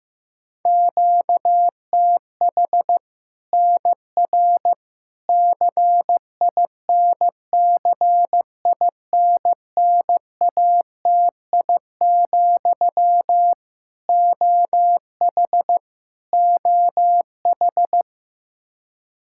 Morse code, heard as a keyed tone, about 15 wpm, 700 Hz. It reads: QTH NR CINCINNATI, OH OH
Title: Practice QSO - Exchange 1 - Beginning - 15wpm